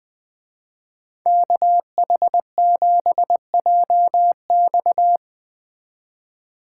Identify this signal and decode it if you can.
Morse code, heard as a keyed tone, about 20 wpm, 700 Hz. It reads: KH7JX